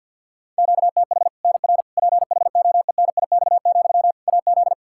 Morse code, heard as a keyed tone, about 52 wpm, 700 Hz. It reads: QTH NR PHOENIX, AZ
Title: Practice QSO - Exchange 1 - Beginning - 50wpm